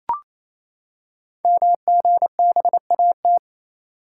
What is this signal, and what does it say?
Morse code, heard as a keyed tone, about 27 wpm, 700 Hz. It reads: MGBAT
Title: Code Groups - 5 Letter:Numbers - 28wpm